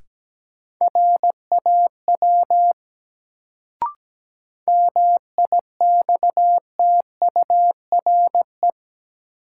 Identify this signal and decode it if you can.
Morse code, heard as a keyed tone, about 17 wpm, 700 Hz. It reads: RAW MIXTURE
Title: Sentences from Top 1500 Words - New Word Drill Alt - 17wpm